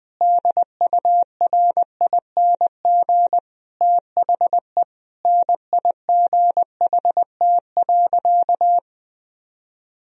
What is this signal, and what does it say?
Morse code, heard as a keyed tone, about 20 wpm, 700 Hz. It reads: DURING THE NIGHT.
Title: Sentences from Top 1000 Words - Review Part 2 - 20wpm